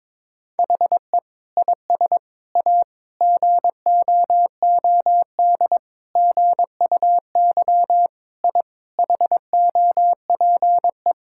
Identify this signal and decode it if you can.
Morse code, heard as a keyed tone, about 21 wpm, 700 Hz. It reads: HE IS A GOOD GUY I HOPE
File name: practice / 22wpm / Sets of 8 Words - 22wpm